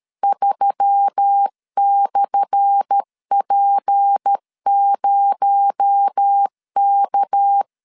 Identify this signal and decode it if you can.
Morse code, heard as a keyed tone, about 13 wpm, 800 Hz. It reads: 3/P0K
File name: LEZIONI / 10 SIGNS / 10B INT 12wpm 2414